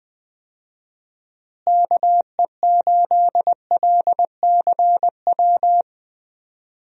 Morse code, heard as a keyed tone, about 20 wpm, 700 Hz. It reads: KE8LCW
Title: USA Call Signs - 6 Characters - Repeated 2x - 20wpm